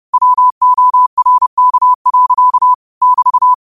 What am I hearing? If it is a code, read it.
Morse code, heard as a keyed tone, about 30 wpm, 1000 Hz. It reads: WORK. <BT>
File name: qotdfiles / 30wpm